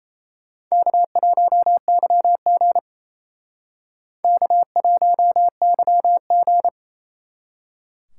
Morse code, heard as a keyed tone, about 34 wpm, 700 Hz. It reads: K1YG K1YG
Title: Speed Racing - USA 1x2 Call Signs 28wpm